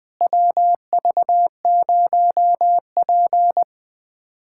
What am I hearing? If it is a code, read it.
Morse code, heard as a keyed tone, about 20 wpm, 700 Hz. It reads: WV0P
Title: USA Callsign DE Callsign - 4 Characters - 20wpm